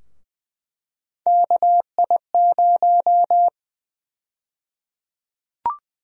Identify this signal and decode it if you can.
Morse code, heard as a keyed tone, about 20 wpm, 700 Hz. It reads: KI0
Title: USA Call Sign Prefixes - 20wpm